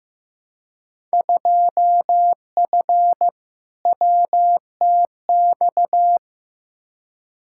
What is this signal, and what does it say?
Morse code, heard as a keyed tone, about 15 wpm, 700 Hz. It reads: 2F WTX
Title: Field Day Exchanges - 15wpm